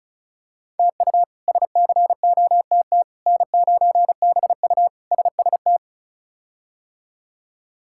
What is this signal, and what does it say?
Morse code, heard as a keyed tone, about 34 wpm, 700 Hz. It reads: TU SCOTT N9BU SST
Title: K1USN SST QSO Practice - 35wpm